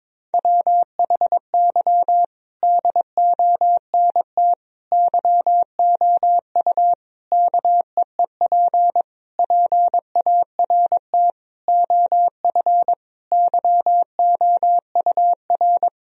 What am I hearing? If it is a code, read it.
Morse code, heard as a keyed tone, about 21 wpm, 700 Hz. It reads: WHY DONT YOU KEEP PART OF YOUR
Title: Sentences from Top 1000 Words - Review Part 4 - 22wpm